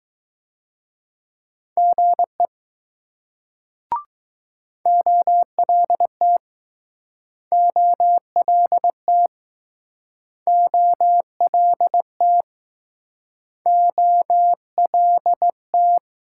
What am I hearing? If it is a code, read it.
Morse code, heard as a keyed tone, about 20 wpm, 700 Hz. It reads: GE OLT OLT OLT OLT